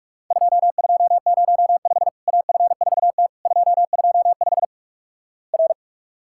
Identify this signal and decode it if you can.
Morse code, heard as a keyed tone, about 44 wpm, 700 Hz. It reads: 2205 AF4T 225 R
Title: Worked All Europe DX Contest - QTC Exchange - 45wpm